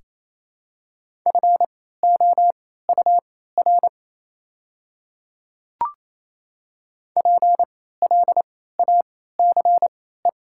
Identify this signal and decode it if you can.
Morse code, heard as a keyed tone, about 28 wpm, spaced as for 16 wpm, 700 Hz. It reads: FOUR PLACE
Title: 3x Character Spacing - Top 300 Words - 28wpm